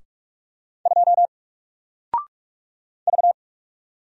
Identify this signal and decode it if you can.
Morse code, heard as a keyed tone, about 47 wpm, 700 Hz. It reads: 2 V
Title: Single Character (Letter:Number:Punctuation:Prosign) Alt 45wpm